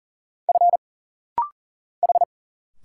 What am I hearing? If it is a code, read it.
Morse code, heard as a keyed tone, about 40 wpm, 700 Hz. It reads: F H
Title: Single Letters - Rapid-Fire 40wpm